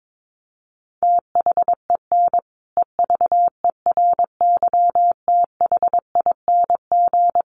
Morse code, heard as a keyed tone, about 22 wpm, 700 Hz. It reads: THEN EVERYTHING
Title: Sentences from Top 700 Words 22wpm